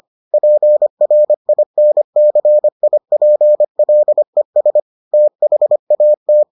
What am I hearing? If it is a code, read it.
Morse code, heard as a keyed tone, about 25 wpm, 600 Hz. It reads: PRINCIPLES THAT